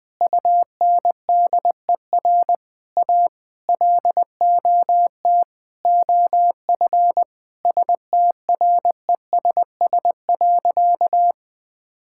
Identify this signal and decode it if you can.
Morse code, heard as a keyed tone, about 21 wpm, 700 Hz. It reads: UNDER A LOT OF STRESS.
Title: Sentences from Top 1000 Words - Review Part 6 - 20wpm